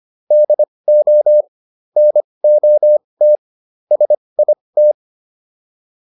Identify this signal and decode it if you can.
Morse code, heard as a keyed tone, about 25 wpm, spaced as for 20 wpm, 600 Hz. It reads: DO NOT SIT